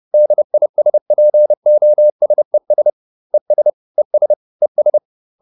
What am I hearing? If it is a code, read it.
Morse code, heard as a keyed tone, about 30 wpm, 600 Hz. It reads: DISPOSES ES ES ES